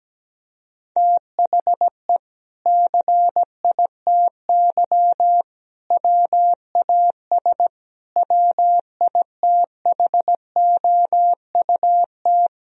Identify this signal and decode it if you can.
Morse code, heard as a keyed tone, about 17 wpm, 700 Hz. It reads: THE CITY WAS WITHOUT
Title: Sentences from Top 1000 Words - Review Part 3 - 17wpm